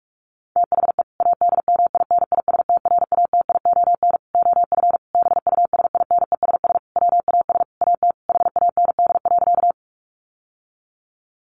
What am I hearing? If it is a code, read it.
Morse code, heard as a keyed tone, about 44 wpm, 700 Hz. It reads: THE ADMINISTRATION OF BUSINESS WAS AT HAND.
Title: Sentences from Top 1300 Words 45wpm